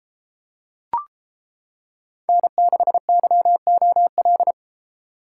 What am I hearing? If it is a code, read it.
Morse code, heard as a keyed tone, about 29 wpm, 700 Hz. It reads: N6YOL